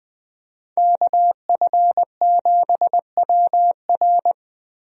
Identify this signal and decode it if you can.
Morse code, heard as a keyed tone, about 20 wpm, 700 Hz. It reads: KF7WR
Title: USA Call Signs - 5 Characters - Repeated 3x - 20wpm